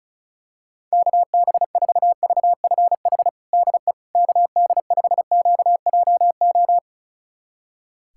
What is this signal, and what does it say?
Morse code, heard as a keyed tone, about 36 wpm, 700 Hz. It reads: KB4VFH DE KD5QJO
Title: USA Callsign DE Callsign 35wpm